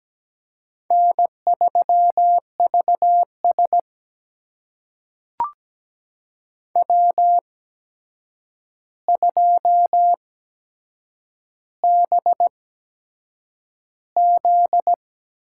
Morse code, heard as a keyed tone, about 17 wpm, spaced as for 7 wpm, 700 Hz. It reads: N3VS W2BZ
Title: USA Call Signs - 4 Characters - 8x Character Spacing - 17wpm